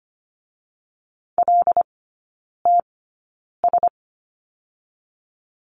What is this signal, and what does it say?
Morse code, heard as a keyed tone, about 25 wpm, spaced as for 9 wpm, 700 Hz. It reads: LTS
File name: Lesson 17e - Letter u - Three Characters 25wpm